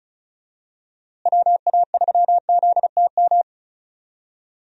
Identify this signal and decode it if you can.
Morse code, heard as a keyed tone, about 36 wpm, 700 Hz. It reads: WA3ZTM
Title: USA Call Signs - 6 Characters - Repeated 2x - 35wpm